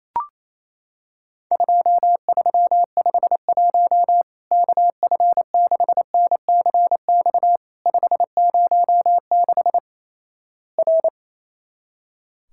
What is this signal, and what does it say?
Morse code, heard as a keyed tone, about 28 wpm, 700 Hz. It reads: 2351 KF6NCX 506 R
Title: Worked All Europe DX Contest - QTC Exchange - 28wpm